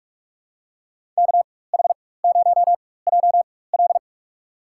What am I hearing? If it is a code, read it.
Morse code, heard as a keyed tone, about 46 wpm, spaced as for 21 wpm, 690 Hz. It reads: KH0JL
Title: USA Call Signs - 5 Characters - 4x Character Spacing - 45wpm